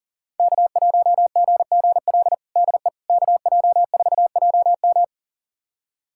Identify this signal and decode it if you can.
Morse code, heard as a keyed tone, about 40 wpm, 700 Hz. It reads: K1GGL DE KJ4JM
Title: USA Callsign DE Callsign - 5 Characters - 40wpm